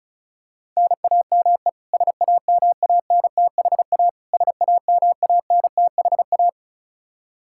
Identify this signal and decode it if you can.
Morse code, heard as a keyed tone, about 36 wpm, 700 Hz. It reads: NAME SAMANTHA SAMANTHA
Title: Practice QSO - Exchange 1 - Beginning - 35wpm